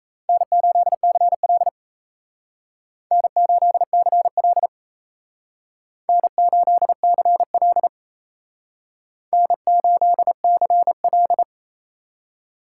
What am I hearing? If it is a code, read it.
Morse code, heard as a keyed tone, about 35 wpm, 700 Hz. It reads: N8CL N8CL N8CL N8CL